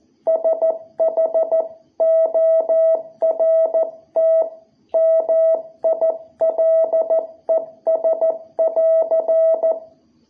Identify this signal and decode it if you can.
Morse code, heard as a keyed tone, about 14 wpm, 660 Hz. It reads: SHORT MILES <AR>
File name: jh8cbh / 2021cw